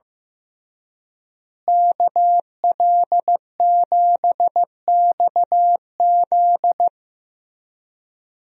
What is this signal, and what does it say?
Morse code, heard as a keyed tone, about 15 wpm, 700 Hz. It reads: KL7XZ
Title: DXCC-100 Call Signs - 5 Characters - Repeated 3x - 15wpm